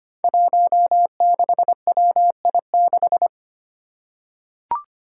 Morse code, heard as a keyed tone, about 25 wpm, 700 Hz. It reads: 16WI6
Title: Five Challenging Letter-Number Combinations 25wpm